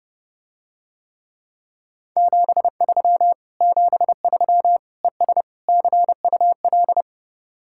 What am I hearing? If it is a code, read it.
Morse code, heard as a keyed tone, about 30 wpm, 700 Hz. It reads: 73 73 ES CUL